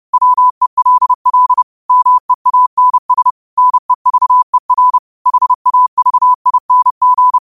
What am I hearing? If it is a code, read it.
Morse code, heard as a keyed tone, about 30 wpm, 1000 Hz. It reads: WELL MEANS NEVER HAVING
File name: qotdfiles / 30wpm